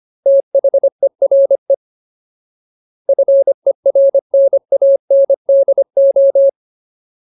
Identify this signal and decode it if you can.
Morse code, heard as a keyed tone, about 25 wpm, 550 Hz. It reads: THERE FERNANDO
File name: Ultimate Word List / 25WPM Ultimate